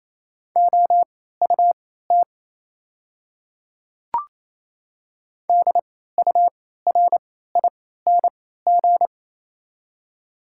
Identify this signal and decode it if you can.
Morse code, heard as a keyed tone, about 28 wpm, spaced as for 16 wpm, 690 Hz. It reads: OUT DURING
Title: 3x Character Spacing - Top 500 Words - 28wpm